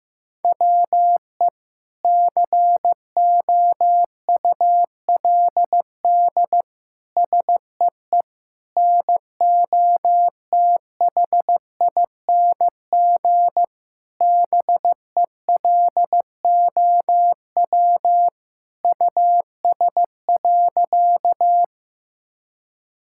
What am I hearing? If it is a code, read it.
Morse code, heard as a keyed tone, about 15 wpm, 700 Hz. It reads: WE COULD SEE NOTHING BELOW US.
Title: Sentences from Top 1000 Words - Review Part 5 - 15wpm